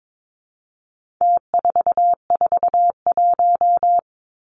Morse code, heard as a keyed tone, about 22 wpm, 700 Hz. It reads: T441
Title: Numbers - Contest Style - 4-Digit 22wpm